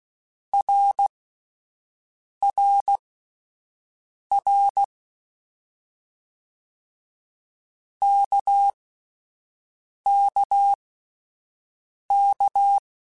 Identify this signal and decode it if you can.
Morse code, heard as a keyed tone, about 16 wpm, spaced as for 5 wpm, 800 Hz. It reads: RRR KKK
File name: Short Runs / Short Runs-mp3s / Balanced @10